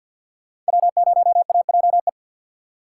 Morse code, heard as a keyed tone, about 62 wpm, 700 Hz. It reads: W0AJE
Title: Speed Racing - USA 1x3 Call Signs 50wpm